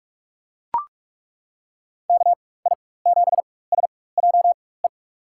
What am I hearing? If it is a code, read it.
Morse code, heard as a keyed tone, about 42 wpm, spaced as for 21 wpm, 700 Hz. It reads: KI7SJE
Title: USA Call Signs - 6 Characters - 4x Character Spacing - 45wpm